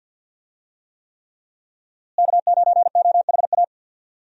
Morse code, heard as a keyed tone, about 50 wpm, 700 Hz. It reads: K9OHA
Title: USA Call Signs - 5 Characters - Repeated 3x - 50wpm